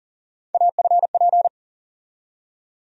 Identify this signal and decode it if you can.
Morse code, heard as a keyed tone, about 40 wpm, 700 Hz. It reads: AFP
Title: USA Call Sign Postfixes - 3 Letters 40wpm